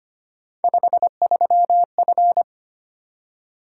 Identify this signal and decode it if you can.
Morse code, heard as a keyed tone, about 25 wpm, 700 Hz. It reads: 53F
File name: Three Letter-Number Combinations - 25wpm